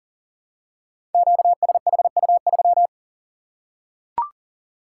Morse code, heard as a keyed tone, about 41 wpm, 700 Hz. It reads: QSHU3
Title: Code Groups - 5 Letter:Numbers - 40wpm